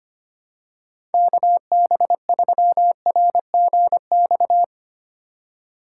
Morse code, heard as a keyed tone, about 25 wpm, 700 Hz. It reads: KB3RGX
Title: USA Call Signs - 6 Characters - 2x Character Spacing - 25wpm